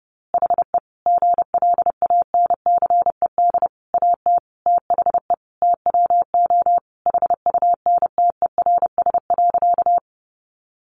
Morse code, heard as a keyed tone, about 30 wpm, 700 Hz. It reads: HE GLANCED AT THE TWO HUNTERS.
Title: A Sound Of Thunder 30wpm